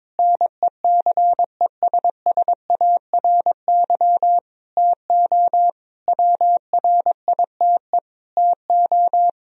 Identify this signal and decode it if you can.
Morse code, heard as a keyed tone, about 22 wpm, 700 Hz. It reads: NECESSARY TO WRITE TO
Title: Sentences from Top 1000 Words - Review Part 1 - 22wpm